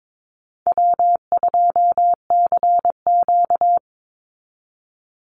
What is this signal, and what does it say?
Morse code, heard as a keyed tone, about 22 wpm, 700 Hz. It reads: W2CQ
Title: Easy Call Signs Repeated 2x - 22wpm